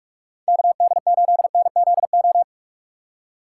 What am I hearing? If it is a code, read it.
Morse code, heard as a keyed tone, about 46 wpm, 700 Hz. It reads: KD8NZO
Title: USA Call Signs - 6 Characters - 2x Character Spacing - 45wpm